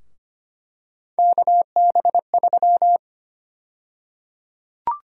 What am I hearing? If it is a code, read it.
Morse code, heard as a keyed tone, about 25 wpm, 700 Hz. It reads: KB3
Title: USA Call Sign Prefixes - 25wpm